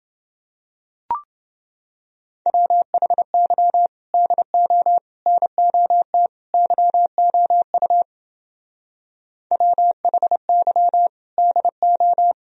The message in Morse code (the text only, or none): WHY DO NOT YOU WHY DO